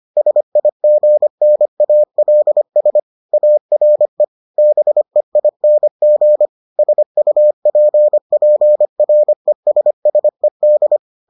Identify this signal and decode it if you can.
Morse code, heard as a keyed tone, about 25 wpm, 600 Hz. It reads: SIGNALS ARE BEING SUPPRESSED